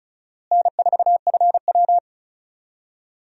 Morse code, heard as a keyed tone, about 34 wpm, 700 Hz. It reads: N4FW
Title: USA Call Signs - 4 Characters - Repeated 3x - 35wpm